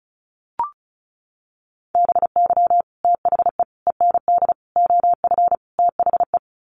Morse code, heard as a keyed tone, about 34 wpm, 700 Hz. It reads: BY THE END OF THE
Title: Sets of 5 Words 35wpm